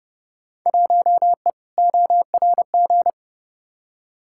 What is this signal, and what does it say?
Morse code, heard as a keyed tone, about 32 wpm, 700 Hz. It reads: 1E ORG
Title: Field Day Exchanges - 30wpm